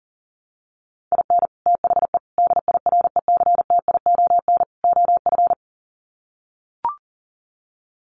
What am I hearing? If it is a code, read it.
Morse code, heard as a keyed tone, about 40 wpm, 700 Hz. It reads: IN THE DIRECTION OF
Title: Sets of 4 Words 40wpm